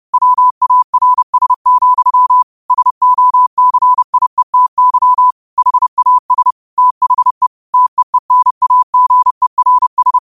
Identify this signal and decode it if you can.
Morse code, heard as a keyed tone, about 30 wpm, 1000 Hz. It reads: WARS, SOCIETY HAS THE TEENAGERS
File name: qotdfiles / 30wpm